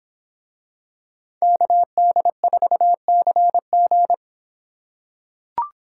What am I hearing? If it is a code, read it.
Morse code, heard as a keyed tone, about 26 wpm, 700 Hz. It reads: KD4CG